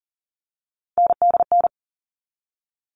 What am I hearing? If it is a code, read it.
Morse code, heard as a keyed tone, about 40 wpm, 700 Hz. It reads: NDN
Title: All Cut Numbers - 3-Digit 40wpm